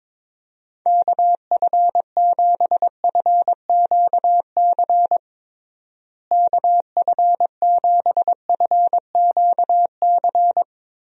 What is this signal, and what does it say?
Morse code, heard as a keyed tone, about 22 wpm, 700 Hz. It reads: KF7FQC KF7FQC
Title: USA Call Signs - 6 Characters - Repeated 2x - 22wpm